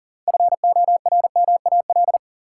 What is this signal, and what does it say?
Morse code, heard as a keyed tone, about 40 wpm, 700 Hz. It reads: FORMAL
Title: Sentences from Top 1300 Words - New Word Drill Alt - 40wpm